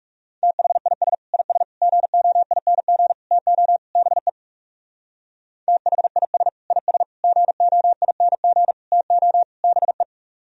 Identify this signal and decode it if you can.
Morse code, heard as a keyed tone, about 36 wpm, 700 Hz. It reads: THIS IS GOING TO BE THIS IS GOING TO BE
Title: Speed-Racing - Sets of 5 Words 30wpm